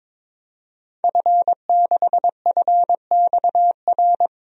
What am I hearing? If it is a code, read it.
Morse code, heard as a keyed tone, about 22 wpm, 700 Hz. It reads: F6FXR
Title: DXCC-100 Call Signs - 5 Characters - Repeated 2x - 22wpm